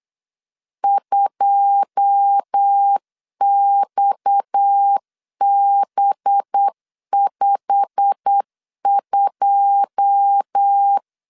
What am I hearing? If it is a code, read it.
Morse code, heard as a keyed tone, about 8 wpm, 800 Hz. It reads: 2XB52
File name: LEZIONI / 9 2 8wpm 3437